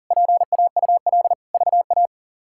Morse code, heard as a keyed tone, about 40 wpm, 700 Hz. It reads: PAUL VA
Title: K1USN SST QSO Practice - 40wpm